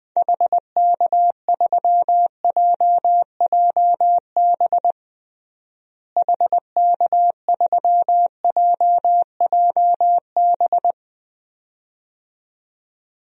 Morse code, heard as a keyed tone, about 20 wpm, 700 Hz. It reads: HK3JJB HK3JJB
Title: DXCC-100 Call Signs - 6 Characters - Repeated 2x - 20wpm